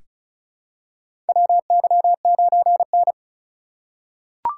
Morse code, heard as a keyed tone, about 36 wpm, 700 Hz. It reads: WY9N
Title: USA Call Signs - 4 Characters - 4x Character Spacing - 35wpm